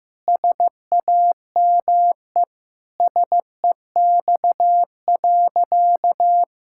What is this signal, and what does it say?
Morse code, heard as a keyed tone, about 16 wpm, 700 Hz. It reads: SAME SEX.
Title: Sentences from Top 1600 Words 15wpm